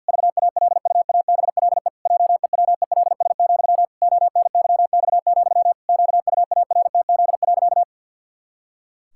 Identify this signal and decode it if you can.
Morse code, heard as a keyed tone, about 50 wpm, 700 Hz. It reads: VALUABLE JEWELS, ONYX, QUARTZ.
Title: Pangram Challenge v2 - 50wpm